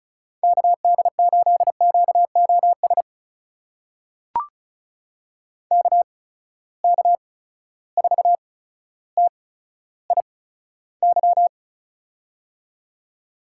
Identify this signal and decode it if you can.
Morse code, heard as a keyed tone, about 36 wpm, spaced as for 15 wpm, 700 Hz. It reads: KD8QOS KK4TIY
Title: USA Call Signs - 8x Character Spacing - 35wpm